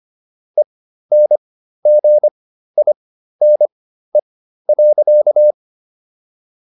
Morse code, heard as a keyed tone, about 25 wpm, spaced as for 13 wpm, 600 Hz. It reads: ENGINE.